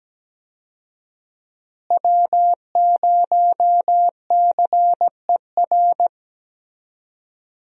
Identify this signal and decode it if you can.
Morse code, heard as a keyed tone, about 17 wpm, 700 Hz. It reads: W0CER
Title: USA Call Signs - 5 Characters - Repeated 3x - 17wpm